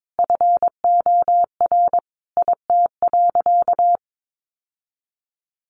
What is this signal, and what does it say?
Morse code, heard as a keyed tone, about 22 wpm, 700 Hz. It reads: FOR IT.
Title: Sentences from Top 1000 Words 22wpm